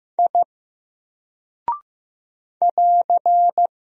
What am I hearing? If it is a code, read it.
Morse code, heard as a keyed tone, about 15 wpm, 700 Hz. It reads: I <AR>
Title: Single Character (Letter:Number:Punctuation:Prosign) Alt 15wpm